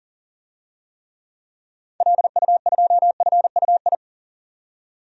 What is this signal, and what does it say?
Morse code, heard as a keyed tone, about 40 wpm, 700 Hz. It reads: LU2FUI
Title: DXCC-100 Call Signs - 6 Characters - 40wpm